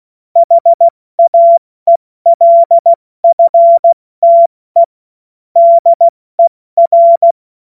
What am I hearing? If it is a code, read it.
Morse code, heard as a keyed tone, about 16 wpm, spaced as for 14 wpm, 680 Hz. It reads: HAELFTE DER
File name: Morsekurs / Witze / Witz 18. Mai 2021, 70 F80